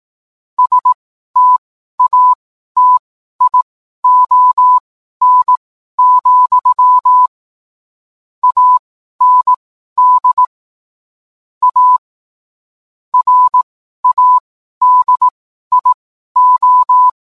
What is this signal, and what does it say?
Morse code, heard as a keyed tone, about 18 wpm, spaced as for 12 wpm, 1000 Hz. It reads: STATION, AND A RADIO
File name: cwblog / cw mp3 directory / mem morse 12